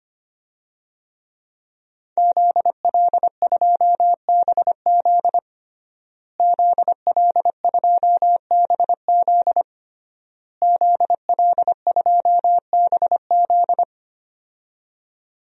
Morse code, heard as a keyed tone, about 25 wpm, 700 Hz. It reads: ZL2BZ ZL2BZ ZL2BZ
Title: DXCC-100 Call Signs - 5 Characters - Repeated 3x - 25wpm